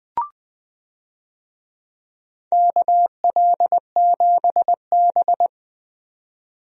Morse code, heard as a keyed tone, about 20 wpm, 700 Hz. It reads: KL7B
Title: DXCC-100 Call Signs - 4 Characters - 20wpm